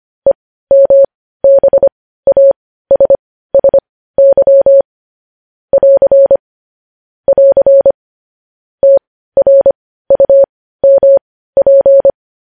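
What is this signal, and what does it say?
Morse code, heard as a keyed tone, about 25 wpm, spaced as for 15 wpm, 550 Hz. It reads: EMBASSY <AR> <AR> TRUMP
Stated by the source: Morse code news headlines sent at 15 wpm.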